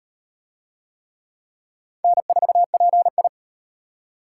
Morse code, heard as a keyed tone, about 38 wpm, 700 Hz. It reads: N4PI